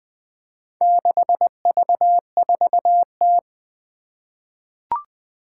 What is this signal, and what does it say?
Morse code, heard as a keyed tone, about 21 wpm, 700 Hz. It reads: 6V4T
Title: Four Challenging Letter-Number Combinations 20wpm